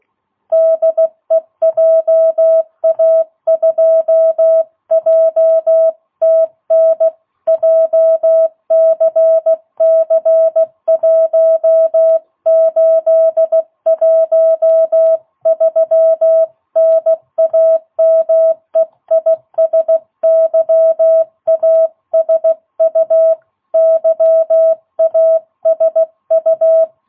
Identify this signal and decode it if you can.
Morse code, heard as a keyed tone, about 16 wpm, 660 Hz. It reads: DE JA2JTN JCC1813 NAME IS YASU YASU
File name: jh8cbh / 2022cw